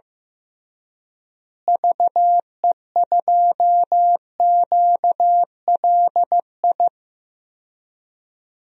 Morse code, heard as a keyed tone, about 15 wpm, 700 Hz. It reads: VE2QLI
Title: DXCC-100 Call Signs - 6 Characters - Repeated 3x - 15wpm